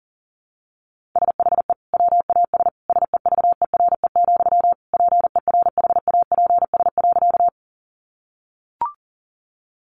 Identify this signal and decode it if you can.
Morse code, heard as a keyed tone, about 40 wpm, 700 Hz. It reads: SHE WAS SEVERE, PERHAPS.
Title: Sentences from Top 1400 Words 40wpm